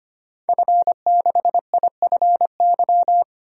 Morse code, heard as a keyed tone, about 25 wpm, 700 Hz. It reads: F6IFY
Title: DXCC-100 Call Signs - 5 Characters - 25wpm